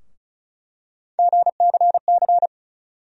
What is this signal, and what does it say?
Morse code, heard as a keyed tone, about 36 wpm, 700 Hz. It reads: GCC
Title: USA Call Sign Postfixes - 3 Letters 35wpm